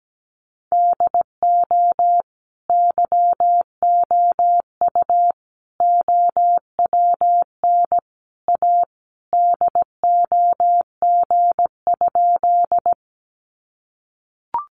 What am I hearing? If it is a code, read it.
Morse code, heard as a keyed tone, about 17 wpm, 700 Hz. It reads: DO YOU OWN A DOG?
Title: Sentences from Top 400 Words 17wpm